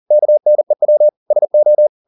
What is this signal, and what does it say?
Morse code, heard as a keyed tone, about 40 wpm, 600 Hz. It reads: KNEW SO